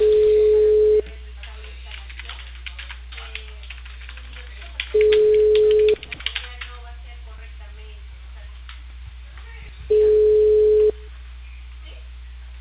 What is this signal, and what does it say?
Ringback tone (Europe, 425 Hz), with speech.